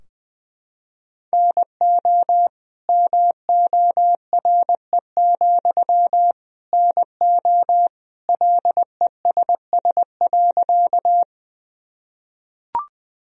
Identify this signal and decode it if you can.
Morse code, heard as a keyed tone, about 20 wpm, 700 Hz. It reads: NO MORE, NO LESS.
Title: Sentences from Top 1000 Words - Review Part 2 - 20wpm